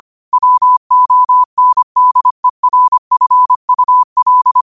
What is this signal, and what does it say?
Morse code, heard as a keyed tone, about 25 wpm, 1000 Hz. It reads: WONDERFUL
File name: qotdfiles / 25wpm